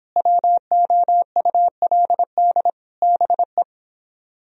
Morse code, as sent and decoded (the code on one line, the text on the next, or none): .-- --- ..- .-.. -.. / -... .
WOULD BE